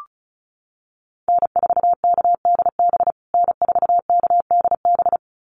Morse code, heard as a keyed tone, about 34 wpm, 700 Hz. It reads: N4KDB N4KDB
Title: USA Call Signs Repeated 2x 35wpm